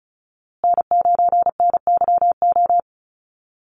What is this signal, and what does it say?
Morse code, heard as a keyed tone, about 34 wpm, 700 Hz. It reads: N9NYO
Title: USA Call Signs Repeated 2x 35wpm